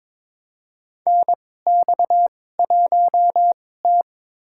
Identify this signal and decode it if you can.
Morse code, heard as a keyed tone, about 21 wpm, spaced as for 16 wpm, 700 Hz. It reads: NX1T
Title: USA Call Signs - 2x Character Spacing - 22wpm